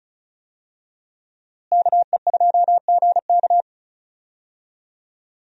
Morse code, heard as a keyed tone, about 36 wpm, 700 Hz. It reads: KE2GK
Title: USA Call Signs - 5 Characters - Repeated 3x - 35wpm